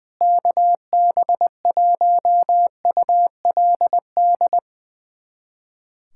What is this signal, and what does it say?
Morse code, heard as a keyed tone, about 20 wpm, 700 Hz. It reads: KB1ULD
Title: USA Callsign DE Callsign - 6 Characters - 20wpm